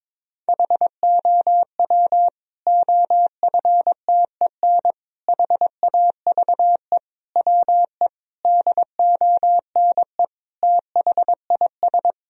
Morse code, heard as a keyed tone, about 22 wpm, 700 Hz. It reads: HOW OFTEN HAVE WE DONE THIS
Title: Sentences from Top 1000 Words - Review Part 1 - 22wpm